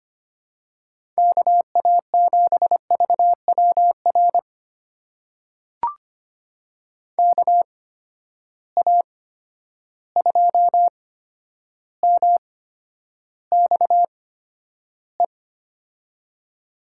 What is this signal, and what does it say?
Morse code, heard as a keyed tone, about 25 wpm, spaced as for 11 wpm, 700 Hz. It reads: KA7VWR KA2MXE
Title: USA Call Signs - 6 Characters - 8x Character Spacing - 25wpm